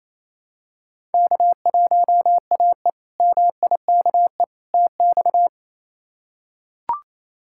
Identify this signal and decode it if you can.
Morse code, heard as a keyed tone, about 28 wpm, 700 Hz. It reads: K1AE MIKE TX
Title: K1USN SST QSO Practice - 28wpm